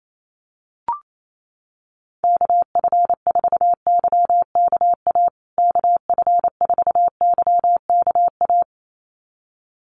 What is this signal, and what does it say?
Morse code, heard as a keyed tone, about 28 wpm, 700 Hz. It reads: KF4YKA KF4YKA
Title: USA Call Signs Repeated 2x 28wpm